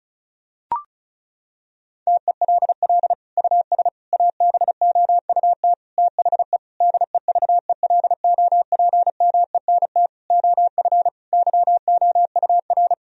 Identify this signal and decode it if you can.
Morse code, heard as a keyed tone, about 36 wpm, 700 Hz. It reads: TELL US ABOUT THE DEVELOPMENT OF YOUR
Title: Sentences from Top 1000 Words - Review Part 4 - 35wpm